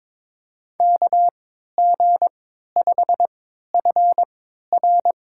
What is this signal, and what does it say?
Morse code, heard as a keyed tone, about 22 wpm, spaced as for 13 wpm, 700 Hz. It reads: KG5FR
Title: USA Call Signs - 3x Character Spacing - 22wpm